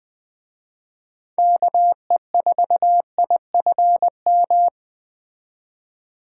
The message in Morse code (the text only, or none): KE4IFM